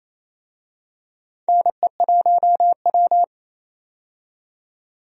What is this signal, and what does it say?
Morse code, heard as a keyed tone, about 28 wpm, 700 Hz. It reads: NE1W
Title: USA Call Signs - 4 Characters - 28wpm